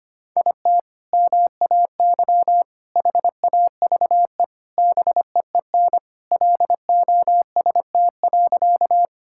Morse code, heard as a keyed tone, about 25 wpm, 700 Hz. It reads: IT MAY HAVE BEEN LOST.
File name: practice / 25wpm / Sentences with 3-Word Phrases - 25wpm